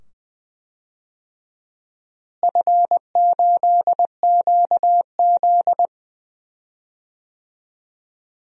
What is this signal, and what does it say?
Morse code, heard as a keyed tone, about 20 wpm, 700 Hz. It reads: F8QZ
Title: DXCC-100 Call Signs - 4 Characters - Repeated 2x - 20wpm